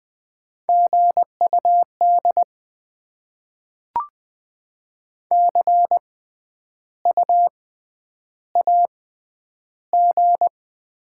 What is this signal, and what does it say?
Morse code, heard as a keyed tone, about 20 wpm, spaced as for 10 wpm, 700 Hz. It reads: GUD CU AG
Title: Top 20 QSO Elements - 6x Character Spacing - 20wpm